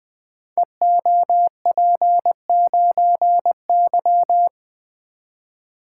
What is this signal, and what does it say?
Morse code, heard as a keyed tone, about 20 wpm, 700 Hz. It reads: EOP9Y
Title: Code Groups - 5 Letter:Numbers - 20wpm